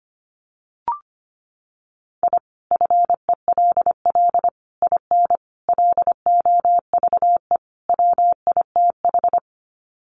Morse code, heard as a keyed tone, about 25 wpm, 700 Hz. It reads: I FELL IN LOVE WITH
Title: Sets of 5 Words Encore 25wpm